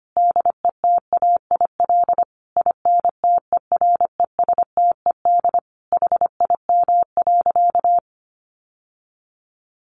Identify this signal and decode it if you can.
Morse code, heard as a keyed tone, about 25 wpm, 700 Hz. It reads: DETAIL INTERESTED HIM.
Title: Sentences from Top 1500 Words 25wpm